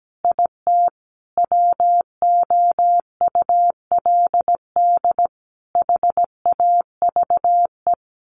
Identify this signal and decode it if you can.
Morse code, heard as a keyed tone, about 17 wpm, 700 Hz. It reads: IT WOULD HAVE
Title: A Sound Of Thunder 17wpm